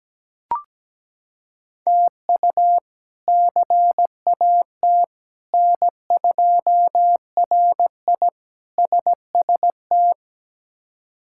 Morse code, heard as a keyed tone, about 17 wpm, 700 Hz. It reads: TU CAT N2RI SST
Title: K1USN SST QSO Practice - 17wpm